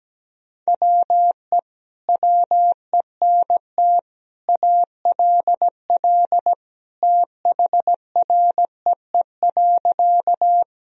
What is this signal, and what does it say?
Morse code, heard as a keyed tone, about 17 wpm, 700 Hz. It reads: WE WENT ALL THREE.
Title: Sentences from Top 1000 Words - Review Part 1 - 17wpm